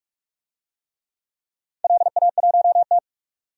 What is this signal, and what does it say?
Morse code, heard as a keyed tone, about 46 wpm, 700 Hz. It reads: LA1T
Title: DXCC-100 Call Signs - 4 Characters - 45wpm